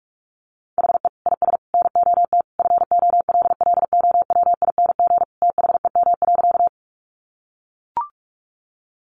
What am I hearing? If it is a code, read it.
Morse code, heard as a keyed tone, about 44 wpm, 700 Hz. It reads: HE IS NOT FOLLOWING THEM.
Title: Sentences from Top 700 Words 45wpm